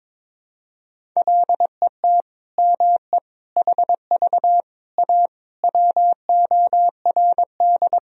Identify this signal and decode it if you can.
Morse code, heard as a keyed tone, about 22 wpm, 700 Hz. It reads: LET ME HV A WORD
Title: Sentences from Top 300 Words and Abbreviations 22wpm